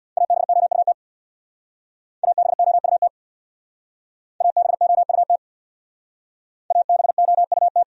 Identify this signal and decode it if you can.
Morse code, heard as a keyed tone, about 64 wpm, 700 Hz. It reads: ABOUT ABOUT ABOUT ABOUT